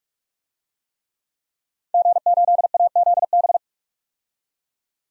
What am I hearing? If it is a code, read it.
Morse code, heard as a keyed tone, about 46 wpm, 700 Hz. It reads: G8AZB
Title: DXCC-100 Call Signs - 5 Characters - 45wpm